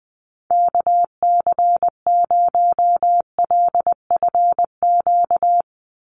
Morse code, heard as a keyed tone, about 19 wpm, 700 Hz. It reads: KC0LFQ
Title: USA Call Signs 20wpm